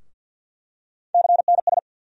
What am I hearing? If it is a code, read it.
Morse code, heard as a keyed tone, about 48 wpm, 700 Hz. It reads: CNS
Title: USA Call Sign Postfixes - 3 Letters 50wpm